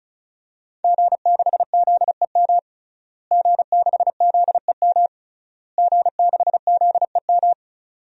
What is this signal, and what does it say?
Morse code, heard as a keyed tone, about 36 wpm, 700 Hz. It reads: G6ZEM G6ZEM G6ZEM
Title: DXCC-100 Call Signs - 5 Characters - Repeated 3x - 35wpm